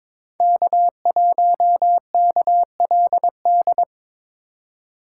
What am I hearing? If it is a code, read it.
Morse code, heard as a keyed tone, about 26 wpm, 700 Hz. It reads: K1KLD
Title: Speed Racing - USA 1x3 Call Signs 22wpm